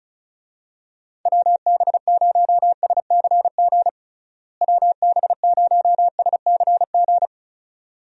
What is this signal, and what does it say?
Morse code, heard as a keyed tone, about 34 wpm, 700 Hz. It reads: WB0SCG WB0SCG
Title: USA Call Signs - 6 Characters - Repeated 2x - 35wpm